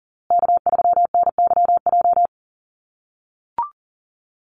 Morse code, heard as a keyed tone, about 40 wpm, 700 Hz. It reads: K3NYJ
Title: USA Call Signs Repeated 3x 40wpm